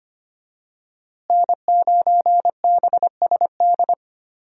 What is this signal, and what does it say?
Morse code, heard as a keyed tone, about 25 wpm, 700 Hz. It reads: N9BSD
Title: USA Call Signs - 5 Characters - Repeated 2x - 25wpm